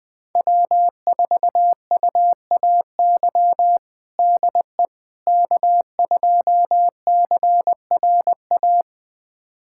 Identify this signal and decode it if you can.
Morse code, heard as a keyed tone, about 20 wpm, 700 Hz. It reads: W4UAY DE K2CRA
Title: USA Callsign DE Callsign - 5 Characters - 20wpm